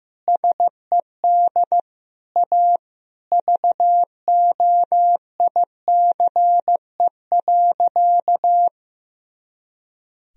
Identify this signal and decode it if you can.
Morse code, heard as a keyed tone, about 15 wpm, 700 Hz. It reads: SED A VOICE.
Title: Sentences from Top 500 Words and Abbreviations 15wpm